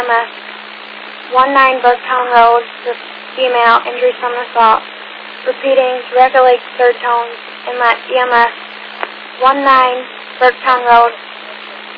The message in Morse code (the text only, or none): none